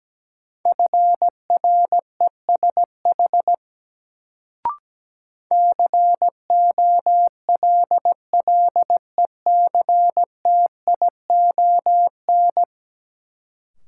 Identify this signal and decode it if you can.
Morse code, heard as a keyed tone, about 17 wpm, 700 Hz. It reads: FRESH COLLECTION
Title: Sentences from Top 1200 Words - New Word Drill Alt - 17wpm